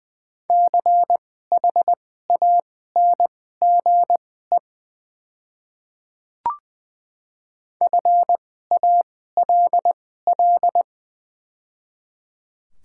Morse code, heard as a keyed tone, about 20 wpm, spaced as for 14 wpm, 700 Hz. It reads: CHANGE FALL
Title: 2x Character Spacing - Top 500 Words - 20wpm